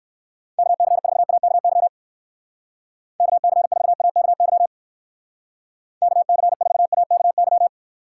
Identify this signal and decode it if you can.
Morse code, heard as a keyed tone, about 64 wpm, 700 Hz. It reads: KC4AKY KC4AKY KC4AKY